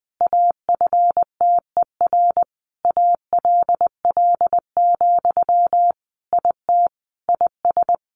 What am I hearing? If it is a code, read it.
Morse code, heard as a keyed tone, about 20 wpm, 700 Hz. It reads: AFTER ALL, IT IS
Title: Sentences from Top 500 Words 20wpm